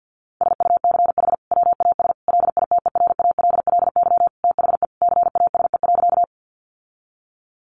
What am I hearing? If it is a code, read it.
Morse code, heard as a keyed tone, about 48 wpm, 700 Hz. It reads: SUCH WAS LITERALLY THE CASE.
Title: Sentences from Top 1400 Words 50wpm